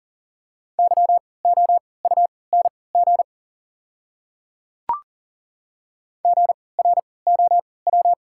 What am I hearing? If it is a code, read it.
Morse code, heard as a keyed tone, about 40 wpm, spaced as for 23 wpm, 690 Hz. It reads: YOUNG GROW
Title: 3x Character Spacing - Top 500 Words - 40wpm